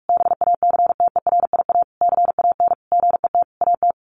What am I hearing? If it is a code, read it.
Morse code, heard as a keyed tone, about 44 wpm, 700 Hz. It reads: BACTERIA CAN GET AT
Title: A Sound Of Thunder 45wpm